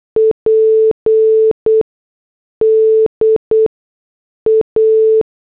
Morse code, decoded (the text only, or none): PDA